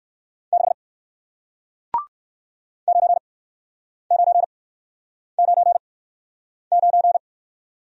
Morse code, heard as a keyed tone, about 58 wpm, spaced as for 9 wpm, 700 Hz. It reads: 6 9999